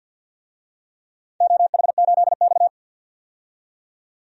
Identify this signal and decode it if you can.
Morse code, heard as a keyed tone, about 49 wpm, 700 Hz. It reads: OH8X
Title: DXCC-100 Call Signs - 4 Characters - 50wpm